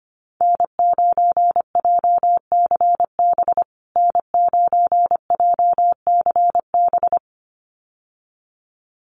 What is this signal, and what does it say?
Morse code, heard as a keyed tone, about 25 wpm, 700 Hz. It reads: N9JCB N9JCB
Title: USA Call Signs Repeated 2x 25wpm